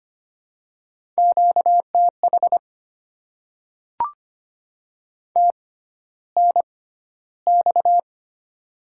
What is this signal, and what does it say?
Morse code, heard as a keyed tone, about 25 wpm, spaced as for 13 wpm, 700 Hz. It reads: QTH TNX
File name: Top 20 QSO Elements - 6x Character Spacing - 25wpm